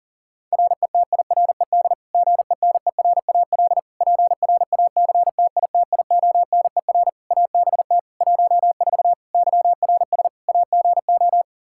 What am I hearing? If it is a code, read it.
Morse code, heard as a keyed tone, about 40 wpm, 700 Hz. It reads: RETIRED GENERAL PRACTITIONER ABT 14 YRS AGO
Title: Practice QSO - Exchange 3 - More Info - 40wpm